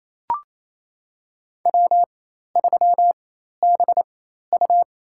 Morse code, heard as a keyed tone, about 28 wpm, spaced as for 13 wpm, 700 Hz. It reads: W3BU
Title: USA Call Signs - 4 Characters - 4x Character Spacing - 28wpm